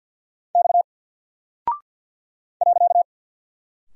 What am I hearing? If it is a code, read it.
Morse code, heard as a keyed tone, about 48 wpm, 700 Hz. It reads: X .
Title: Single Character (Letter:Number:Punctuation:Prosign) Alt 50wpm